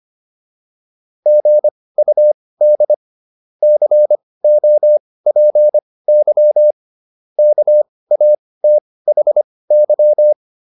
Morse code, heard as a keyed tone, about 25 wpm, spaced as for 18 wpm, 600 Hz. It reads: GUD COPY KATHY
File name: QSO307 18